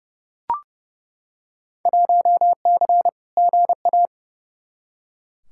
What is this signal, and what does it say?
Morse code, heard as a keyed tone, about 29 wpm, 700 Hz. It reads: 1C GA
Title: Field Day Exchanges - 30wpm